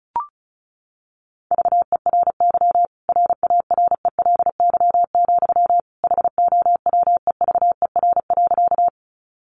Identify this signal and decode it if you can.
Morse code, heard as a keyed tone, about 34 wpm, 700 Hz. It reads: VERY RARELY, HOWEVER.
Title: Sentences from Top 1400 Words 35wpm